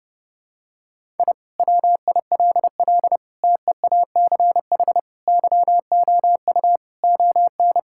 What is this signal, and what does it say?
Morse code, heard as a keyed tone, about 30 wpm, 700 Hz. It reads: I WILL TEACH YOU ON
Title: Sentences from Top 1000 Words - Review Part 6 - 30wpm